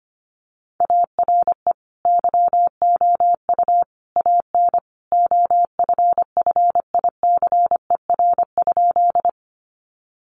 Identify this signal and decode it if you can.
Morse code, heard as a keyed tone, about 25 wpm, 700 Hz. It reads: ARE YOU AN OFFICER?
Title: Sentences from Top 1200 Words 25wpm